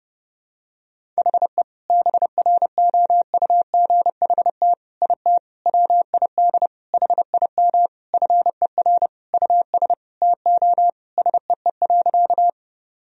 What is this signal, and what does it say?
Morse code, heard as a keyed tone, about 28 wpm, 700 Hz. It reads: HE BROUGHT IT WID HIM FER US TO SEE.
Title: Sentences from Top 500 Words and Abbreviations 30wpm